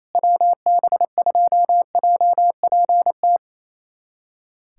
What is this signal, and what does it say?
Morse code, heard as a keyed tone, about 28 wpm, 700 Hz. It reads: WB2JPT
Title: USA Callsign DE Callsign 28wpm